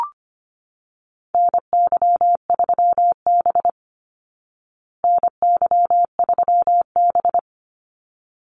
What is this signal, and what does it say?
Morse code, heard as a keyed tone, about 25 wpm, 700 Hz. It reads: NY3B NY3B
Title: Call Signs Repeated 3x - 25wpm